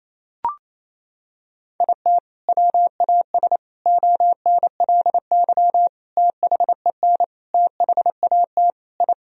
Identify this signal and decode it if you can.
Morse code, heard as a keyed tone, about 28 wpm, 700 Hz. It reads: IT WAS ONLY THEN THAT I
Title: Sentences from Top 1000 Words - Review Part 3 - 28wpm